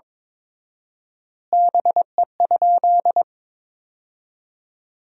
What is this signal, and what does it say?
Morse code, heard as a keyed tone, about 21 wpm, 700 Hz. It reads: BE?
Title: 4x Word-Spacing - Sentences from Top 100 Words - 22wpm